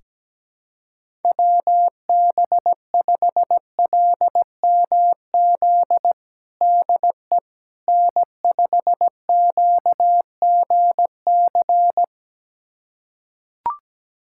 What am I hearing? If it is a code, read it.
Morse code, heard as a keyed tone, about 17 wpm, 700 Hz. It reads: WB5LMZ DE N5QGC
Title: USA Callsign DE Callsign 17wpm